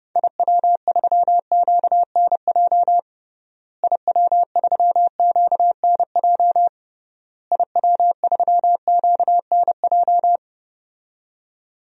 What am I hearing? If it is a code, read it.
Morse code, heard as a keyed tone, about 30 wpm, 700 Hz. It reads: IW3QNJ IW3QNJ IW3QNJ
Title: DXCC-100 Call Signs - 6 Characters - Repeated 3x - 30wpm